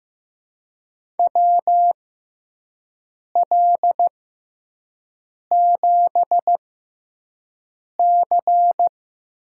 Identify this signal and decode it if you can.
Morse code, heard as a keyed tone, about 15 wpm, spaced as for 5 wpm, 700 Hz. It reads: WL7C
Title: USA Call Signs - 6 Characters - 6x Character Spacing - 15wpm